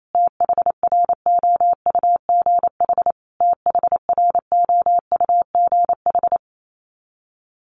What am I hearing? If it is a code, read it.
Morse code, heard as a keyed tone, about 28 wpm, 700 Hz. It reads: THROUGH THROUGH
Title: Top 200 Words Repeated 28wpm